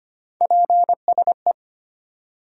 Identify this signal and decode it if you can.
Morse code, heard as a keyed tone, about 25 wpm, 700 Hz. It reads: PSE
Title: Top 90 QSO Elements 25wpm